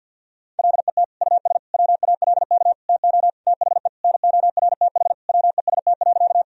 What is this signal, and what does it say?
Morse code, heard as a keyed tone, about 50 wpm, 700 Hz. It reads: LET US WALK TO THE NORTH WEST.
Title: Sentences from Top 1000 Words - Review Part 2 - 50wpm